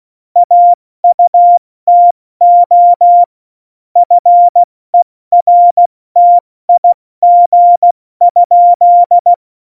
Morse code, heard as a keyed tone, about 16 wpm, spaced as for 14 wpm, 700 Hz. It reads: AUTO FERTIG?
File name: Morsekurs / Witze / Witz 02 Juni 2020, 70 F80